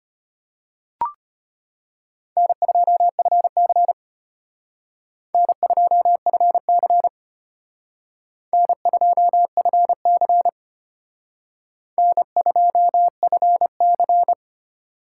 Morse code, heard as a keyed tone, about 32 wpm, 700 Hz. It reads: N2FC N2FC N2FC N2FC